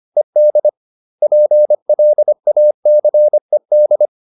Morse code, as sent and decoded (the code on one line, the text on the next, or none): . -.. / .--. .-.. .- -.-. -..
ED PLACED